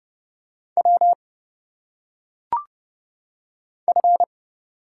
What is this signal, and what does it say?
Morse code, heard as a keyed tone, about 31 wpm, 700 Hz. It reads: W F
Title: Single Letters - Phonetic Pronunciation 30wpm